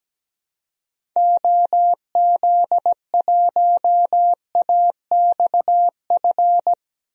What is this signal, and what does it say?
Morse code, heard as a keyed tone, about 17 wpm, 700 Hz. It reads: OZ1AXF
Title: DXCC-100 Call Signs - 6 Characters - 17wpm